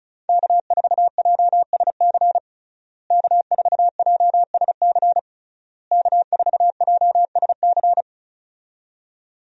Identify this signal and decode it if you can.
Morse code, heard as a keyed tone, about 36 wpm, 700 Hz. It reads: K4JSC K4JSC K4JSC
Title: USA Call Signs - 5 Characters - Repeated 3x - 35wpm